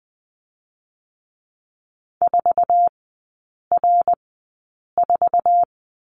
Morse code, heard as a keyed tone, about 20 wpm, spaced as for 8 wpm, 700 Hz. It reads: 4R4
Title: 060 Lesson 11e - Letter h - Three Characters 20wpm